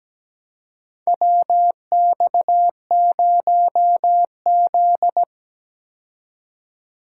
Morse code, heard as a keyed tone, about 17 wpm, 700 Hz. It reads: WX0Z
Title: K1USN SST QSO Practice - 17wpm